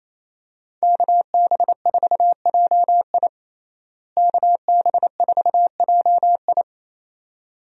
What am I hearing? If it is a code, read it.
Morse code, heard as a keyed tone, about 27 wpm, 700 Hz. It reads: KB4JI KB4JI
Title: USA Call Signs - 5 Characters - Repeated 2x - 28wpm